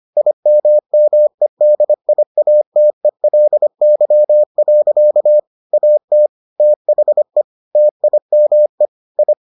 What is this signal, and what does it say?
Morse code, heard as a keyed tone, about 25 wpm, 600 Hz. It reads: IMMEDIATELY. AT THE TIME I